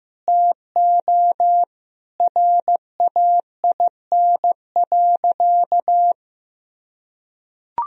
Morse code, heard as a keyed tone, about 15 wpm, 700 Hz. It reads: TO RAIN.
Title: Sentences from Top 1000 Words - Review Part 1 - 15wpm